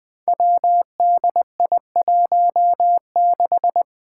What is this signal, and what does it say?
Morse code, heard as a keyed tone, about 20 wpm, 700 Hz. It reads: WDI16
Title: Five Challenging Letter-Number Combinations 20wpm